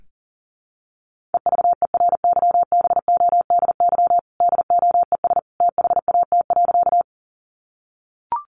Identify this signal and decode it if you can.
Morse code, heard as a keyed tone, about 40 wpm, 700 Hz. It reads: EVERYBODY DOES THAT.
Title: Sentences from Top 1400 Words 40wpm